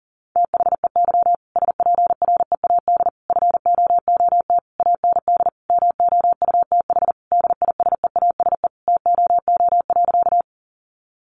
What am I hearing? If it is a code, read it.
Morse code, heard as a keyed tone, about 40 wpm, 700 Hz. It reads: THEY SPREAD FOOT AND MOUTH DISEASE TOO.
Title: Sentences from Top 1500 Words 40wpm